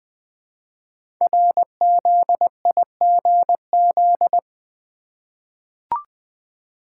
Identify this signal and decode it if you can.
Morse code, heard as a keyed tone, about 20 wpm, 700 Hz. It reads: RZIGZ
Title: Code Groups - 5 Letters - 20wpm